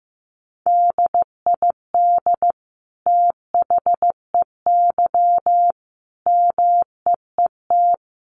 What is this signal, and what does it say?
Morse code, heard as a keyed tone, about 15 wpm, 700 Hz. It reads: DID THEY MEET
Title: Sentences from Top 1100 Words 15wpm